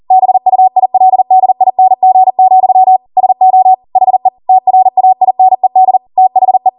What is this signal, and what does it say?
Morse code, heard as a keyed tone, about 40 wpm, 750 Hz. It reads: BUILDING, SO HE TRAINED THE